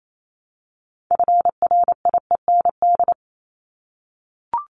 Morse code, heard as a keyed tone, about 28 wpm, 700 Hz. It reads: FRIEND
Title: Top 500 Words 28wpm